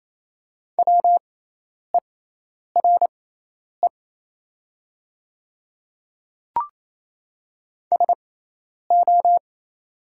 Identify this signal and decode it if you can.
Morse code, heard as a keyed tone, about 27 wpm, spaced as for 10 wpm, 700 Hz. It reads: WERE SO
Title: 6x Character Spacing - Top 100 Words - 28wpm